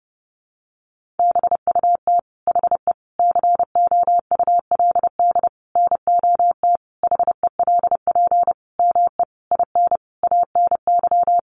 Morse code, heard as a keyed tone, about 30 wpm, 700 Hz. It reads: BUT HE COULD NOT HELP ME IN ANY
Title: Sentences from Top 600 Words 30wpm